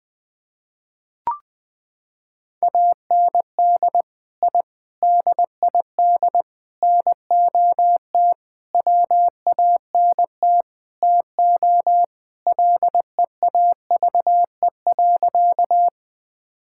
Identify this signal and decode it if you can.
Morse code, heard as a keyed tone, about 20 wpm, 700 Hz. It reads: AND I DID NOT WANT TO LEAVE.
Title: Sentences with 5-Word Phrases - 20wpm